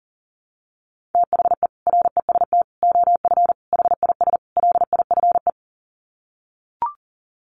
Morse code, heard as a keyed tone, about 40 wpm, 700 Hz. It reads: THE REST OF HIS LIFE
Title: Sets of 5 Words 40wpm